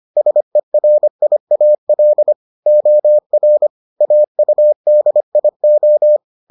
Morse code, heard as a keyed tone, about 25 wpm, 600 Hz. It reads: SERIAL OR AUDIO